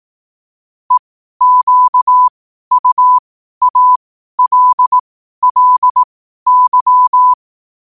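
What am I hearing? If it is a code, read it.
Morse code, heard as a keyed tone, about 18 wpm, spaced as for 12 wpm, 1000 Hz. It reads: EQUALLY
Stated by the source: Morse code at 12 wpm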